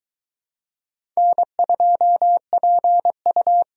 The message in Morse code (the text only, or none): N2PU